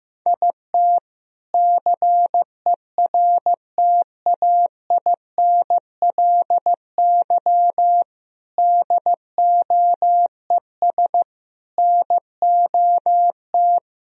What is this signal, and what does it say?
Morse code, heard as a keyed tone, about 15 wpm, 700 Hz. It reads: IT CERTAINLY DOES NOT
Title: Sentences from Top 1000 Words - Review Part 1 - 15wpm